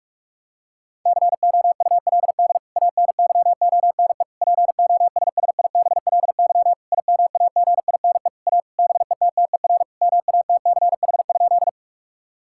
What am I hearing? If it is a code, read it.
Morse code, heard as a keyed tone, about 46 wpm, 700 Hz. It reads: COULD ANYONE POSSIBLY IMAGINE A BETTER MATCH?
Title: Sentences from Top 1600 Words 45wpm